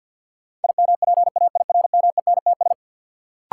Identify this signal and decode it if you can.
Morse code, heard as a keyed tone, about 51 wpm, 700 Hz. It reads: IMPAIRMENTS
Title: Sets of 5 Words Encore - New Word Drill - 50wpm